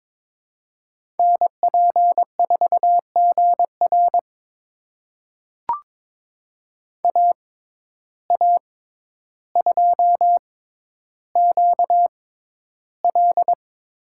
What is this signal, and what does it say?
Morse code, heard as a keyed tone, about 22 wpm, spaced as for 11 wpm, 700 Hz. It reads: NP4GR AA2QL
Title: USA Call Signs - 5 Characters - 6x Character Spacing - 22wpm